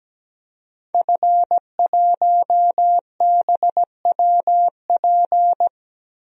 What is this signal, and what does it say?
Morse code, heard as a keyed tone, about 17 wpm, 700 Hz. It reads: F1BWP
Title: DXCC-100 Call Signs - 5 Characters - Repeated 2x - 17wpm